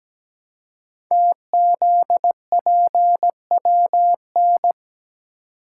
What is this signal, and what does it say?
Morse code, heard as a keyed tone, about 17 wpm, 700 Hz. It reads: TZPWN
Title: Code Groups - 5 Letter:Numbers - 17wpm